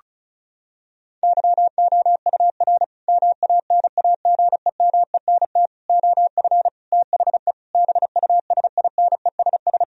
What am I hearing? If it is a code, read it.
Morse code, heard as a keyed tone, about 36 wpm, 700 Hz. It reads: YOUR MANAGEMENT OF THE BUSINESS
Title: Sentences from Top 1000 Words - Review Part 4 - 35wpm